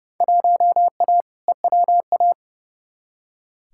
Morse code, heard as a keyed tone, about 32 wpm, 700 Hz. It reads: 1A EWA
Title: Field Day Exchanges Encore - 30wpm